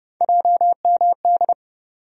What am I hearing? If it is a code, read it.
Morse code, heard as a keyed tone, about 32 wpm, 700 Hz. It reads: JMD
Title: USA Call Sign Postfixes - 3 Letters 30wpm